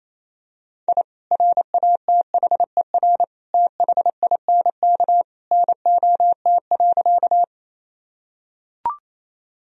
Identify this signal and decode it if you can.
Morse code, heard as a keyed tone, about 27 wpm, 700 Hz. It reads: I RATHER THINK NOT.
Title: Sentences from Top 1000 Words - Review Part 4 - 28wpm